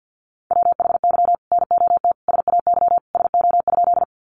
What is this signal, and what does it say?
Morse code, heard as a keyed tone, about 48 wpm, 700 Hz. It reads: WHY NOT SAY SO?
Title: Sentences from Top 600 Words 50wpm